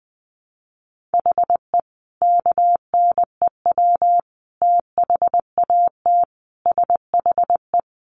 Morse code, heard as a keyed tone, about 19 wpm, 700 Hz. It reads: HE KNEW THAT SHE
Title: Sentences from Top 1100 Words 20wpm